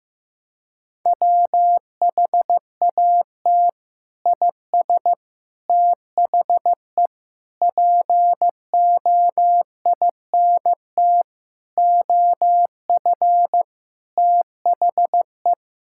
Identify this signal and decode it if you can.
Morse code, heard as a keyed tone, about 15 wpm, 700 Hz. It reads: WHAT IS THE POINT OF THE
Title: Sentences from Top 1000 Words - Review Part 5 - 15wpm